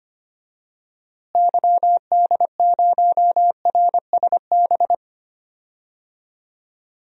Morse code, heard as a keyed tone, about 25 wpm, 700 Hz. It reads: YD0RSB
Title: DXCC-100 Call Signs - 6 Characters - Repeated 3x - 25wpm